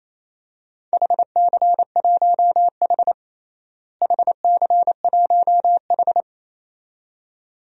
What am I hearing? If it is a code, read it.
Morse code, heard as a keyed tone, about 27 wpm, 700 Hz. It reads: HC1H HC1H
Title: DXCC-100 Call Signs - 4 Characters - Repeated 2x - 28wpm